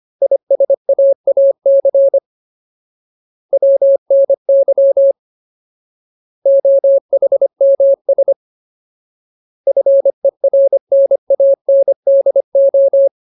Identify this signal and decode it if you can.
Morse code, heard as a keyed tone, about 25 wpm, 550 Hz. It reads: ISAAC WNY OHMS FERNANDO
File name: Ultimate Word List / 25WPM Ultimate